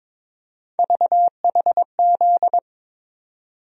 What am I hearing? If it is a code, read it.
Morse code, heard as a keyed tone, about 22 wpm, 700 Hz. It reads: VHZ
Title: USA Call Sign Postfixes - 3 Letters 22wpm